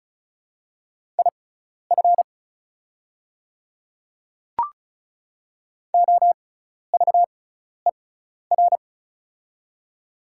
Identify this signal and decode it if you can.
Morse code, heard as a keyed tone, about 34 wpm, spaced as for 12 wpm, 700 Hz. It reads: IF OVER
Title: 6x Character Spacing - Top 200 Words - 35wpm